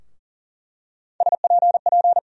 Morse code, heard as a keyed tone, about 38 wpm, 700 Hz. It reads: SPP
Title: USA Call Sign Postfixes - 3 Letters 40wpm